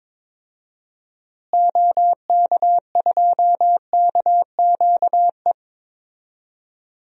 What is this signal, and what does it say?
Morse code, heard as a keyed tone, about 22 wpm, 700 Hz. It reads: OK2KQE
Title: DXCC-100 Call Signs - 6 Characters - Repeated 2x - 22wpm